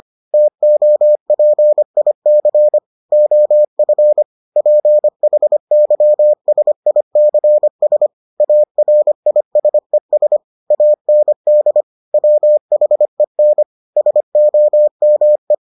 Morse code, heard as a keyed tone, about 25 wpm, 600 Hz. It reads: TOPIC OF PHYSICS ARISES AND WHEN SOME